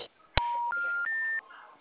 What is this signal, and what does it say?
Special information tone, with speech.